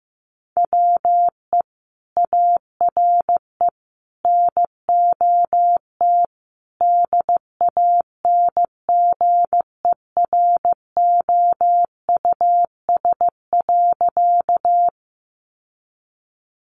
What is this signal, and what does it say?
Morse code, heard as a keyed tone, about 15 wpm, 700 Hz. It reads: WE ARE NOT DANGEROUS.
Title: Sentences from Top 1200 Words 15wpm